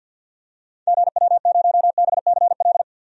Morse code, heard as a keyed tone, about 50 wpm, 700 Hz. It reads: GW0BCL
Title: DXCC-100 Call Signs - 6 Characters - 50wpm